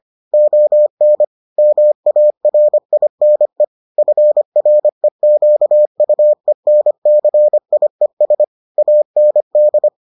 Morse code, heard as a keyed tone, about 25 wpm, 600 Hz. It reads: ON MARINE FREQUENCIES AND